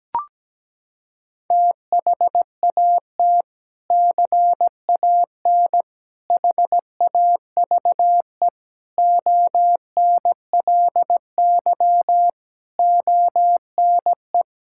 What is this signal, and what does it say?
Morse code, heard as a keyed tone, about 17 wpm, 700 Hz. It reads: THAT CAN HAVE ONLY ONE
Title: Sentences from Top 1000 Words - Review Part 6 - 17wpm